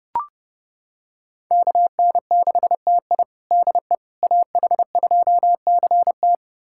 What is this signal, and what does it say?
Morse code, heard as a keyed tone, about 30 wpm, 700 Hz. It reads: KN6TI DE AH2CT
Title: USA Callsign DE Callsign - 5 Characters - 30wpm